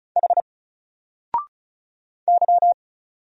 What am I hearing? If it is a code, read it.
Morse code, heard as a keyed tone, about 35 wpm, 700 Hz. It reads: H Y
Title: Single Letter-Number 35wpm